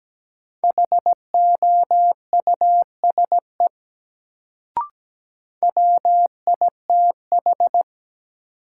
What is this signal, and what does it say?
Morse code, heard as a keyed tone, about 17 wpm, 700 Hz. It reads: HOUSE WITH
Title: Sets of 2 Words - New Word Drill - 17wpm